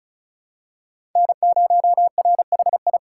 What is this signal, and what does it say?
Morse code, heard as a keyed tone, about 36 wpm, 700 Hz. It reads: N0RHI
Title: USA Call Signs - 5 Characters - 3x Character Spacing - 35wpm